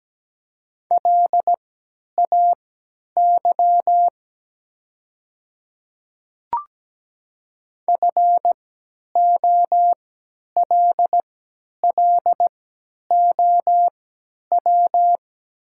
Morse code, heard as a keyed tone, about 17 wpm, spaced as for 10 wpm, 700 Hz. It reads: LAY FOLLOW
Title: 3x Character Spacing - Top 500 Words - 17wpm